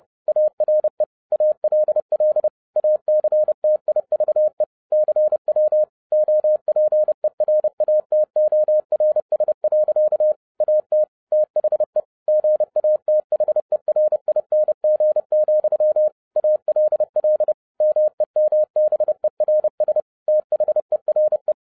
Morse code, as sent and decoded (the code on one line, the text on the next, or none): .- .-. / .- .-.. .-.. / .- -.-. - .. ...- . / -.-. .-- / --- .--. .-. .- - --- .-. ... .-.-.- / .- - / - .... / --. .- - .... .-. .. -. --. --..-- / .- .-.. .-.. / -- . -- -... .-. ... / - .... .-. .
ARE ALL ACTIVE CW OPERATORS. AT THE GATHERING, ALL MEMBERS THERE